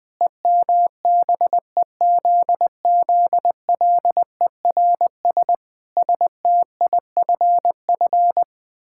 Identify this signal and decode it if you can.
Morse code, heard as a keyed tone, about 20 wpm, 700 Hz. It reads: EMBEZZLERS STIFF
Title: Pangram Challenge v2 - 20wpm